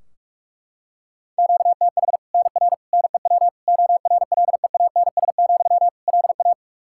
Morse code, heard as a keyed tone, about 41 wpm, 700 Hz. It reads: QTH NR NEW ORLEANS, LA
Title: K1USN SST QSO Practice 45wpm